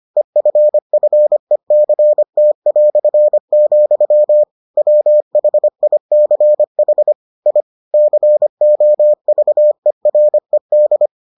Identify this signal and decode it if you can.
Morse code, heard as a keyed tone, about 25 wpm, 600 Hz. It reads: EFFECT", WHICH I COVERED